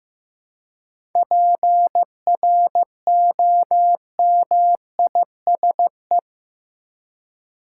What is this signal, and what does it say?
Morse code, heard as a keyed tone, about 15 wpm, 700 Hz. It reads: PROMISE
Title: Seven Letter Words - 15wpm